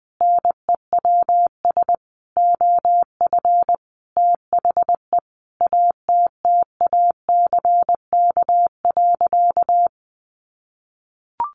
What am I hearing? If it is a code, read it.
Morse code, heard as a keyed tone, about 20 wpm, 700 Hz. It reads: NEWS OF THE ATTACK.
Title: Sentences from Top 1500 Words 20wpm